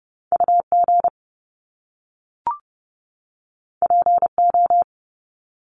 Morse code, heard as a keyed tone, about 30 wpm, 700 Hz. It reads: UG PO
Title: Top Two Letter Combinations in Words 30wpm